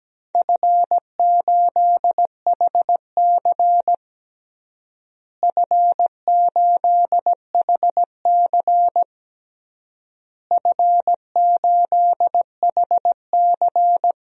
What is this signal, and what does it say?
Morse code, heard as a keyed tone, about 17 wpm, 700 Hz. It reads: F8HC F8HC F8HC
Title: DXCC-100 Call Signs - 4 Characters - Repeated 3x - 17wpm